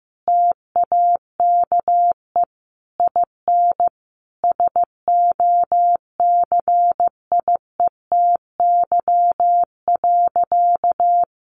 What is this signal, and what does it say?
Morse code, heard as a keyed tone, about 15 wpm, 700 Hz. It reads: TAKE IN SOCIETY.
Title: Sentences from Top 1500 Words 15wpm